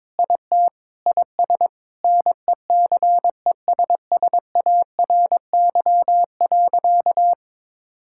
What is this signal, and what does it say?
Morse code, heard as a keyed tone, about 22 wpm, 700 Hz. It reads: IT IS NECESSARY.
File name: Sentences with 9-Word Phrases - 22wpm